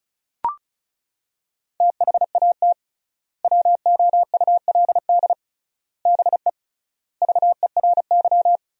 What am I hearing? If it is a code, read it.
Morse code, heard as a keyed tone, about 34 wpm, 700 Hz. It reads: THAT WOULD BE VERY
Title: 3x Word-Spacing - Sentences from Top 200 Words - 35wpm